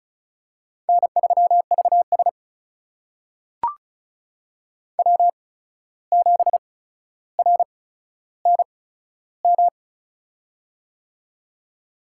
Morse code, heard as a keyed tone, about 36 wpm, spaced as for 14 wpm, 700 Hz. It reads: N3VS W7RNM
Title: USA Call Signs - 8x Character Spacing - 35wpm